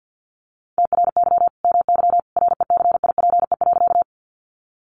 Morse code, heard as a keyed tone, about 50 wpm, 700 Hz. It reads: TRY MY RECIPE.
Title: Sentences from Top 1200 Words 50wpm